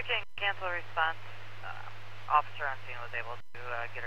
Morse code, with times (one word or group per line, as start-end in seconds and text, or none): none